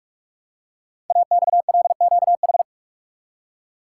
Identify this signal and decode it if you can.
Morse code, heard as a keyed tone, about 44 wpm, 690 Hz. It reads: AXLQH
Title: Code Groups - 5 Letters - 45wpm